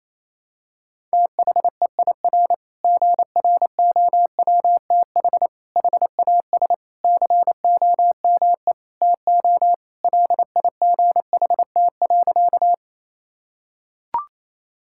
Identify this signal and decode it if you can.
Morse code, heard as a keyed tone, about 28 wpm, 700 Hz. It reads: THEIR GROWTH HAS COME TO LIGHT.
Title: Sentences from Top 1000 Words - Review Part 5 - 28wpm